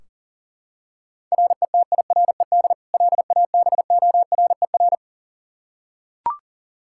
Morse code, heard as a keyed tone, about 40 wpm, 700 Hz. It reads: RETIRED LABORER
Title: Practice QSO - Exchange 3 - More Info - 40wpm